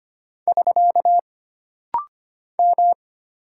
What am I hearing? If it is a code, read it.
Morse code, heard as a keyed tone, about 25 wpm, 700 Hz. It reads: <SK> M
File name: Single Character (Letter:Number:Punctuation:Prosign) - Rapid-Fire - 25wpm